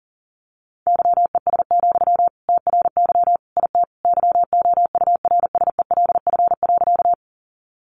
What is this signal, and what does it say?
Morse code, heard as a keyed tone, about 40 wpm, 700 Hz. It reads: YES, TRY IT YOURSELF.
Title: Sentences from Top 600 Words 40wpm